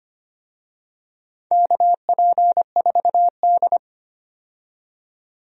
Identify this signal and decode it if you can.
Morse code, heard as a keyed tone, about 25 wpm, 700 Hz. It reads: KP4D
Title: DXCC-100 Call Signs - 4 Characters - 25wpm